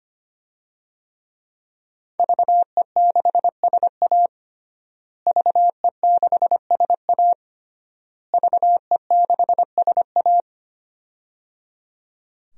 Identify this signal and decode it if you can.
Morse code, heard as a keyed tone, about 25 wpm, 700 Hz. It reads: VE6SA VE6SA VE6SA
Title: DXCC-100 Call Signs - 5 Characters - Repeated 3x - 25wpm